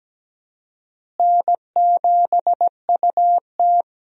Morse code, heard as a keyed tone, about 17 wpm, 700 Hz. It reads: N7UT
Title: USA Call Signs - 4 Characters - Repeated 2x - 17wpm